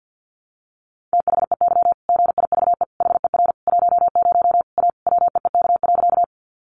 Morse code, heard as a keyed tone, about 48 wpm, 700 Hz. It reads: THEY GIVE HER 10 A WEEK.
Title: Easy Phrases 50wpm